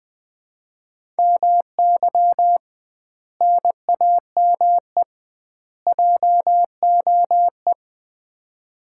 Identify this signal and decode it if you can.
Morse code, heard as a keyed tone, about 20 wpm, 700 Hz. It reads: MY NAME JOE
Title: Common QSO Phrases - 2x Word-Spacing - 20wpm